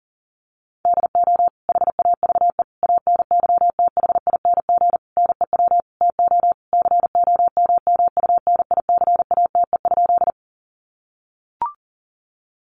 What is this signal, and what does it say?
Morse code, heard as a keyed tone, about 40 wpm, 700 Hz. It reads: DO HAVE ANYTHING NEW TO COMMUNICATE?
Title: Sentences from Top 1300 Words 40wpm